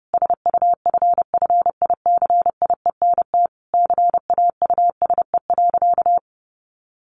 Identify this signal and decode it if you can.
Morse code, heard as a keyed tone, about 30 wpm, 700 Hz. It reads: SUFFICIENT CAUSE.
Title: Sentences from Top 1400 Words 30wpm